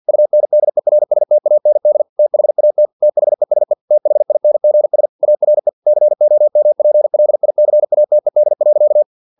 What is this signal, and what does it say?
Morse code, heard as a keyed tone, about 50 wpm, 600 Hz. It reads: UNDERSTAND THAT THESE THINGS ARE COMPLICATED.
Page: Never 49wpm